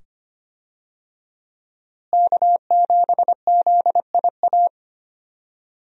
Morse code, heard as a keyed tone, about 25 wpm, 700 Hz. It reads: K7ZIA
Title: USA Call Signs - 5 Characters - Repeated 2x - 25wpm